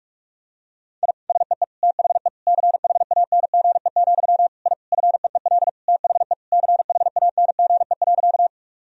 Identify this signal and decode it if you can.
Morse code, heard as a keyed tone, about 46 wpm, 700 Hz. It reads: I SEE THE CHANGE, I FEEL THE CHANGE.
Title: Sentences from Top 1000 Words - Review Part 1 - 45wpm